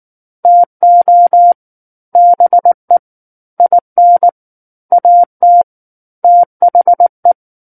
Morse code, heard as a keyed tone, about 19 wpm, 700 Hz. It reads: TO BE IN AT THE
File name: Book One / 10-Chapter 10 - 19 WPM